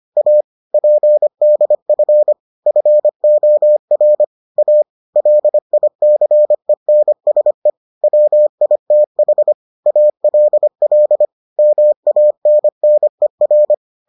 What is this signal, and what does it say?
Morse code, heard as a keyed tone, about 25 wpm, 600 Hz. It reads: A PDF FOR A LICENSE WITH ALL MANNER